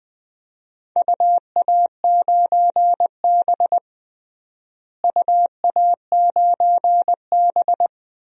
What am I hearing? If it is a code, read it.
Morse code, heard as a keyed tone, about 20 wpm, 700 Hz. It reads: UA9B UA9B
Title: DXCC-100 Call Signs - 4 Characters - Repeated 2x - 20wpm